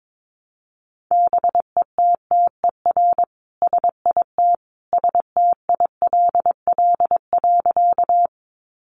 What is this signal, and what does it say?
Morse code, heard as a keyed tone, about 22 wpm, 700 Hz. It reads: BETTER SIT STILL.
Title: Sentences from Top 500 Words 22wpm